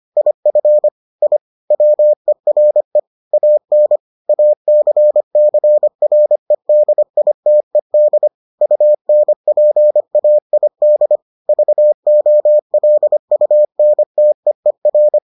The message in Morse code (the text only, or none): IF I WERE AN ACCREDITED UNPAID VOLUNTEER